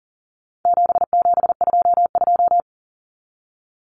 Morse code, heard as a keyed tone, about 40 wpm, 700 Hz. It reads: 7722
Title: Numbers - 4-Digit 40wpm